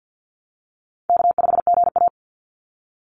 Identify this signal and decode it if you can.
Morse code, heard as a keyed tone, about 50 wpm, 700 Hz. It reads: K5GA
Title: Easy Call Signs Repeated 2x - 50wpm